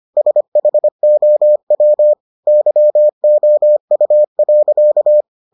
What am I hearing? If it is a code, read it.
Morse code, heard as a keyed tone, about 25 wpm, 600 Hz. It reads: SHOW YOU.